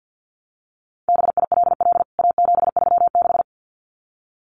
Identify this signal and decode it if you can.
Morse code, heard as a keyed tone, about 50 wpm, 700 Hz. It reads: BILL A736
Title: CWT Exchange 50wpm